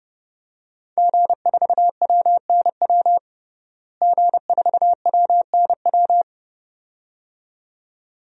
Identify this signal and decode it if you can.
Morse code, heard as a keyed tone, about 30 wpm, 700 Hz. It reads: G4WNW G4WNW
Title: DXCC-100 Call Signs - 5 Characters - Repeated 2x - 30wpm